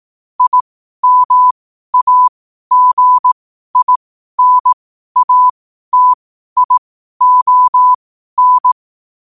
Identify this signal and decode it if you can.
Morse code, heard as a keyed tone, about 18 wpm, spaced as for 12 wpm, 1000 Hz. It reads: IMAGINATION
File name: qotdfiles / 12wpm